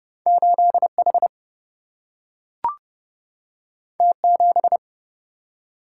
Morse code, heard as a keyed tone, about 30 wpm, 700 Hz. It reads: 8H T7
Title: Single-Letter Single-Number 30wpm